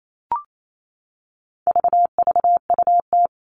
Morse code, heard as a keyed tone, about 28 wpm, 700 Hz. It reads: VVUT
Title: All Cut Numbers - 4-Digit 28wpm